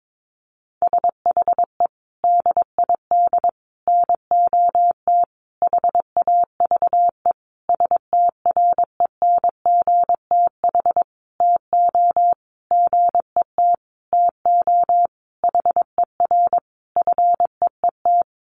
Morse code, heard as a keyed tone, about 22 wpm, 700 Hz. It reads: SHE DID NOT HAVE STRENGTH TO GET TO HER FEET
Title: Sentences from Top 1300 Words 22wpm